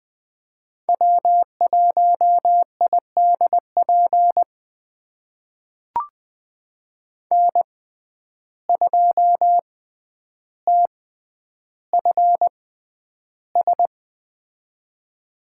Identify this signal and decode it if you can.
Morse code, heard as a keyed tone, about 20 wpm, spaced as for 10 wpm, 700 Hz. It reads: W1IDP N2TFS
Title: USA Call Signs - 5 Characters - 6x Character Spacing - 20wpm